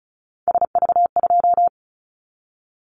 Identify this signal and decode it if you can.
Morse code, heard as a keyed tone, about 36 wpm, 690 Hz. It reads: SV2
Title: Callsign Prefixes for Top 50 DXCC - 35wpm